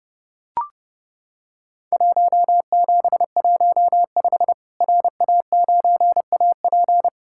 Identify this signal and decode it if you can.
Morse code, heard as a keyed tone, about 30 wpm, 700 Hz. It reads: 1715 RA9AP
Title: Worked All Europe DX Contest - QTC Exchange - 30wpm